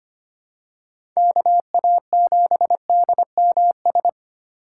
Morse code, heard as a keyed tone, about 25 wpm, 700 Hz. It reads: KA7DMS
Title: USA Call Signs - 6 Characters - 4x Character Spacing - 25wpm